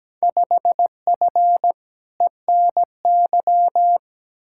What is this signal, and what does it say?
Morse code, heard as a keyed tone, about 17 wpm, 700 Hz. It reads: 5F ENY
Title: Field Day Exchanges Encore - 17wpm